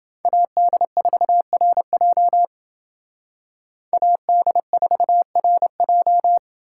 Morse code, heard as a keyed tone, about 28 wpm, 700 Hz. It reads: AD4RJ AD4RJ